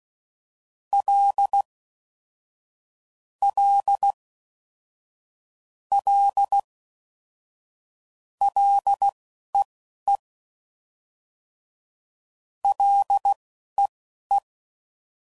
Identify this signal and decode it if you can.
Morse code, heard as a keyed tone, about 16 wpm, spaced as for 11 wpm, 800 Hz. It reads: L L L LEE LEE
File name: Prefixes / FLs & THs / L Prefixes